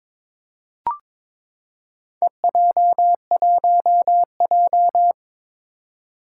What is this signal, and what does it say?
Morse code, heard as a keyed tone, about 22 wpm, 700 Hz. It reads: EJ1J
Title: Four Challenging Letter-Number Combinations 22wpm